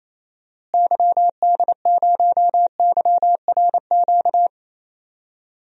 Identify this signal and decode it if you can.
Morse code, heard as a keyed tone, about 28 wpm, 700 Hz. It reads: YD0YRQ
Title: DXCC-100 Call Signs - 6 Characters - Repeated 3x - 28wpm